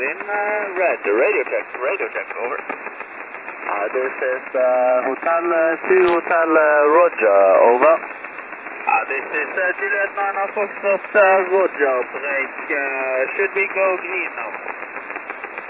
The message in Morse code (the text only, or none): none